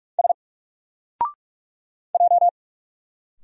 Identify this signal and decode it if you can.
Morse code, heard as a keyed tone, about 44 wpm, 700 Hz. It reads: S J
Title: Single Character (Letter:Number:Punctuation:Prosign) 45wpm